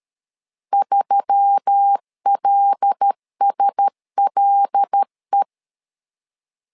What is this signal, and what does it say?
Morse code, heard as a keyed tone, about 13 wpm, 800 Hz. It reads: 3LSLE
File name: LEZIONI / 5 LBVC / 5 L 12wpm 1913